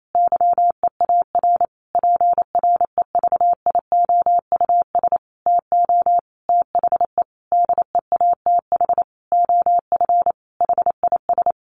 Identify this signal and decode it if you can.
Morse code, heard as a keyed tone, about 28 wpm, 700 Hz. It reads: YEAR PREVIOUS TO THE DEATH OF HIS
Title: Sentences from Top 1500 Words 28wpm